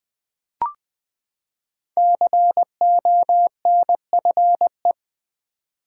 Morse code, heard as a keyed tone, about 20 wpm, 700 Hz. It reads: CONFE
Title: Build A Word - 5 Letters At A Time - 10-Letter Words - 20wpm